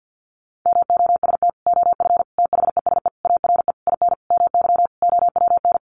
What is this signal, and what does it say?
Morse code, heard as a keyed tone, about 50 wpm, 700 Hz. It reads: MOST OF THESE ARE IN MY OWN
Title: Sentences from Top 1200 Words 50wpm